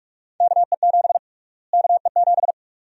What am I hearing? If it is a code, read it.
Morse code, heard as a keyed tone, about 42 wpm, 700 Hz. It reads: KE7 KE7
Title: USA Call Sign Prefixes - Repeated 2x - 45wpm